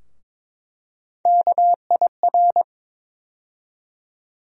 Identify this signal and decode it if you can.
Morse code, heard as a keyed tone, about 22 wpm, 700 Hz. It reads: KIR
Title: USA Call Sign Postfixes - 3 Letters 22wpm